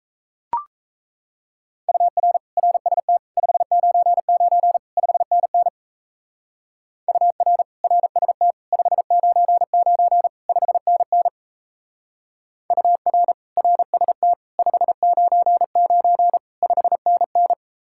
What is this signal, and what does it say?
Morse code, heard as a keyed tone, about 36 wpm, 700 Hz. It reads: UR RST 599 5NN UR RST 599 5NN UR RST 599 5NN